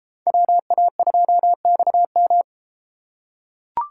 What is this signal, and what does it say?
Morse code, heard as a keyed tone, about 31 wpm, 700 Hz. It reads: WA2XM